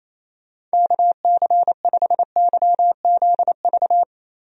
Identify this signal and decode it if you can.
Morse code, heard as a keyed tone, about 28 wpm, 700 Hz. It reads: KC5YZV
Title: USA Call Signs - 6 Characters - Repeated 3x - 28wpm